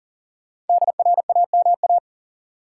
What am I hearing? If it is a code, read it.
Morse code, heard as a keyed tone, about 40 wpm, 700 Hz. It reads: DRAMA
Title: Sentences from Top 1500 Words - New Word Drill Alt - 40wpm